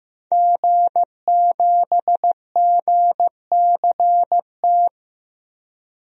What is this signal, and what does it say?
Morse code, heard as a keyed tone, about 15 wpm, 700 Hz. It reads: G7GCT
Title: DXCC-100 Call Signs - 5 Characters - 15wpm